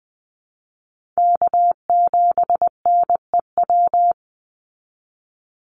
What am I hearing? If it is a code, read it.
Morse code, heard as a keyed tone, about 20 wpm, 700 Hz. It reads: K7NEW
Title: NA SOTA Chaser Callsigns 20wpm